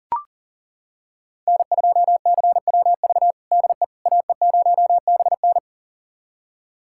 Morse code, heard as a keyed tone, about 40 wpm, 700 Hz. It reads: N2CWV DE AE0BN
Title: USA Callsign DE Callsign - 5 Characters - 40wpm